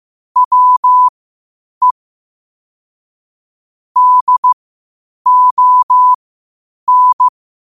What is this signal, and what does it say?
Morse code, heard as a keyed tone, about 15 wpm, spaced as for 8 wpm, 1000 Hz. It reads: WE DON
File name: qotdfiles / 8wpm